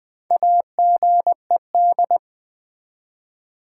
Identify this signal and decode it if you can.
Morse code, heard as a keyed tone, about 20 wpm, 700 Hz. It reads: AGED
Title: Build A Word - 1 Letter At A Time - 4-Letter Words - 20wpm